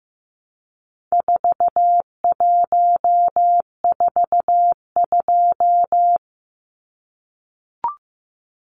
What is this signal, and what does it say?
Morse code, heard as a keyed tone, about 15 wpm, 700 Hz. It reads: 4142
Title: Numbers - 4-Digit 15wpm